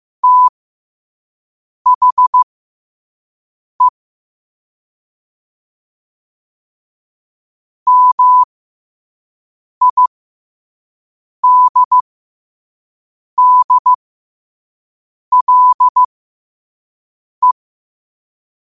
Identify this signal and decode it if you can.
Morse code, heard as a keyed tone, about 15 wpm, spaced as for 5 wpm, 1000 Hz. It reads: THE MIDDLE
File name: qotdfiles / 5wpm